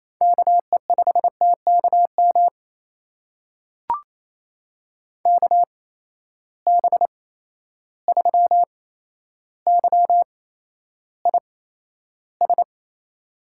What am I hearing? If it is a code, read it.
Morse code, heard as a keyed tone, about 28 wpm, spaced as for 12 wpm, 700 Hz. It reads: KE5TKM KB3YIS
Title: USA Call Signs - 6 Characters - 8x Character Spacing - 28wpm